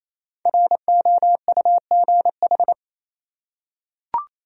Morse code, heard as a keyed tone, about 28 wpm, 700 Hz. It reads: ROUGH
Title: Five Letter Words - 28wpm